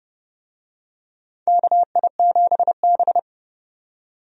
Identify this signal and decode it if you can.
Morse code, heard as a keyed tone, about 30 wpm, 700 Hz. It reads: KI7B
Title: USA Call Signs - 4 Characters - 30wpm